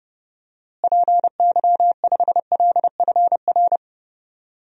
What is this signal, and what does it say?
Morse code, heard as a keyed tone, about 31 wpm, 700 Hz. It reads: PY5LFR
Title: DXCC-100 Call Signs - 6 Characters - Repeated 2x - 30wpm